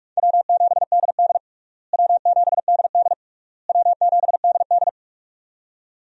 Morse code, heard as a keyed tone, about 46 wpm, 700 Hz. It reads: W7DD W7DD W7DD
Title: USA Call Signs - 4 Characters - Repeated 3x - 45wpm